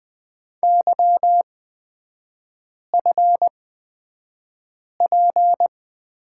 Morse code, heard as a keyed tone, about 20 wpm, spaced as for 5 wpm, 700 Hz. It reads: YFP
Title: 224 Lesson 36e - BT - Three Characters 20wpm